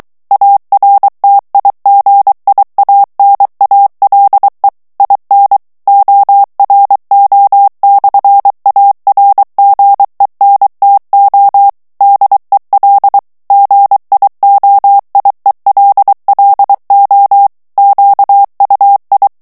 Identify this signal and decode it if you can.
Morse code, heard as a keyed tone, about 23 wpm, 800 Hz. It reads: ARTIGIANALE IN ORO/ARGENTO DEL GIOIELLO QUI
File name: morse mp3 / ITALIEN texte clair 1 vitesse 22 mots minute